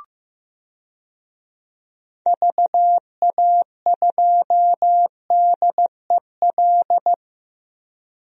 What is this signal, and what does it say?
Morse code, heard as a keyed tone, about 15 wpm, 700 Hz. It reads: VA2DEL
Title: DXCC-100 Call Signs - 6 Characters - 15wpm